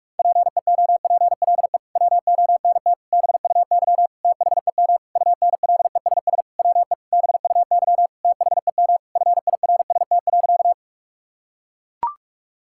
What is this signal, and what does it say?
Morse code, heard as a keyed tone, about 41 wpm, 700 Hz. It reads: PEOPLE WONT BUY THEM UNLESS WE BUY THEM FIRST.
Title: Sentences from Top 1000 Words - Review Part 6 - 45wpm